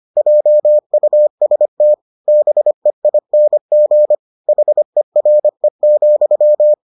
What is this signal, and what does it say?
Morse code, heard as a keyed tone, about 25 wpm, 600 Hz. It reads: JUST BEING HERE,